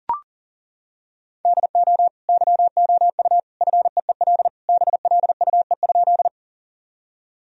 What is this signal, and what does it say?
Morse code, heard as a keyed tone, about 40 wpm, 700 Hz. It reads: DO YOU FEEL BLUE?
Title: Sentences from Top 1000 Words - Review Part 3 - 40wpm